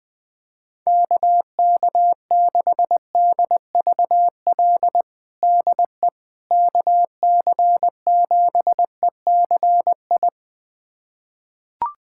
Morse code, heard as a keyed tone, about 20 wpm, 700 Hz. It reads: KK6DVL DE KC7ECI
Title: USA Callsign DE Callsign - 6 Characters - 20wpm